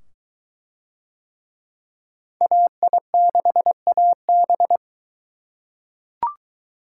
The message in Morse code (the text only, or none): AI6AB